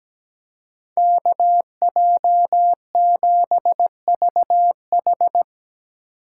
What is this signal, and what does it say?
Morse code, heard as a keyed tone, about 20 wpm, 700 Hz. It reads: KJ7VH
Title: Speed Racing - USA 2x2 Call Signs 17wpm